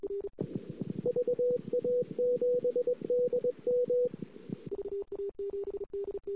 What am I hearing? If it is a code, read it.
Morse code, heard as a keyed tone, about 21 wpm, 490 Hz. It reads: VA7DM
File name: CW / 2008-ss-cw-k5tr-op-n5rz -22:00